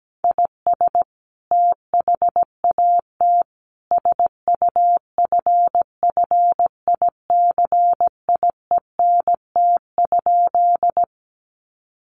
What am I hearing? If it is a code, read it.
Morse code, heard as a keyed tone, about 17 wpm, 700 Hz. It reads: IS THAT SUFFICIENT?
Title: Sentences from Top 1400 Words 17wpm